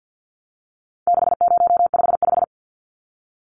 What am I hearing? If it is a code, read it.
Morse code, heard as a keyed tone, about 50 wpm, 700 Hz. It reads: 6055
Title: Numbers - 4-Digit 50wpm